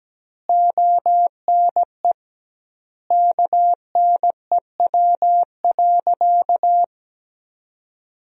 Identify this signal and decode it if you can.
Morse code, heard as a keyed tone, about 17 wpm, 700 Hz. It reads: ONE KNEW.
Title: 2x Word-Spacing - Sentences from Top 400 Words - 17wpm